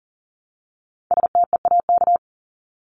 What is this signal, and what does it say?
Morse code, heard as a keyed tone, about 40 wpm, 700 Hz. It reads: STEAK
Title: Top 1251-1500 Words 40wpm